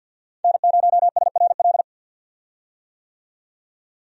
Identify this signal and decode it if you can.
Morse code, heard as a keyed tone, about 48 wpm, 700 Hz. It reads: N0SRL
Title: USA Call Signs - 5 Characters - 50wpm